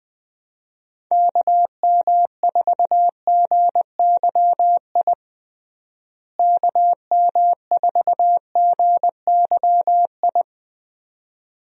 Morse code, heard as a keyed tone, about 20 wpm, 700 Hz. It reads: KM4GYI KM4GYI
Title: USA Call Signs - 6 Characters - Repeated 2x - 20wpm